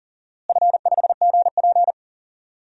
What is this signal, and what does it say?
Morse code, heard as a keyed tone, about 40 wpm, 700 Hz. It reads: F5GP
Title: DXCC-100 Call Signs - 4 Characters - Repeated 2x - 40wpm